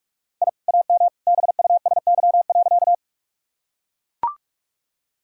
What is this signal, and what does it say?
Morse code, heard as a keyed tone, about 45 wpm, 700 Hz. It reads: I AM BUSY.
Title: Sentences from Top 1000 Words - Review Part 3 - 45wpm